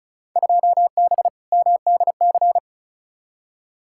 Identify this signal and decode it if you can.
Morse code, heard as a keyed tone, about 34 wpm, 700 Hz. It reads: 2B MDC
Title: Field Day Exchanges - 35wpm